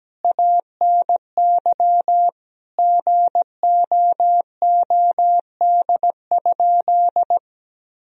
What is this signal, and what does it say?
Morse code, heard as a keyed tone, about 17 wpm, 700 Hz. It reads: ANY GOOD?
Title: Sentences from Top 1000 Words - Review Part 3 - 17wpm